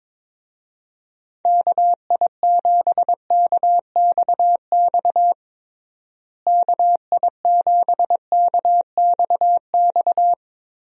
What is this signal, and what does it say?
Morse code, heard as a keyed tone, about 22 wpm, 700 Hz. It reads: KI7KXX KI7KXX
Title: USA Call Signs - 6 Characters - Repeated 3x - 22wpm